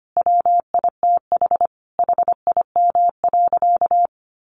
Morse code, heard as a keyed tone, about 25 wpm, 700 Hz. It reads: WITH HIM.
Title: Sentences from Top 1400 Words 25wpm